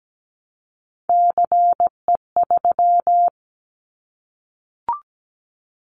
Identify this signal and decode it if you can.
Morse code, heard as a keyed tone, about 17 wpm, 700 Hz. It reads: CE3
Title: Callsign Prefixes for Top 100 DXCC - 17wpm